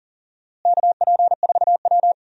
Morse code, heard as a keyed tone, about 40 wpm, 700 Hz. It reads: KP4W
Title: DXCC-100 Call Signs - 4 Characters - Repeated 3x - 40wpm